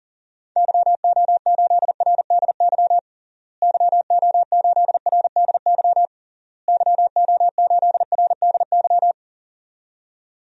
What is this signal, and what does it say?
Morse code, heard as a keyed tone, about 40 wpm, 700 Hz. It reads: YO8RDY YO8RDY YO8RDY
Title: DXCC-100 Call Signs - 6 Characters - Repeated 3x - 40wpm